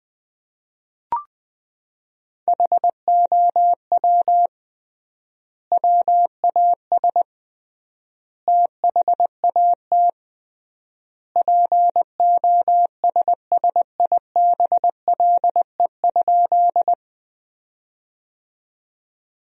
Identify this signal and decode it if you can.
Morse code, heard as a keyed tone, about 20 wpm, 700 Hz. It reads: HOW WAS THAT POSSIBLE?
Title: 3x Word-Spacing - Sentences from Top 500 Words - 20wpm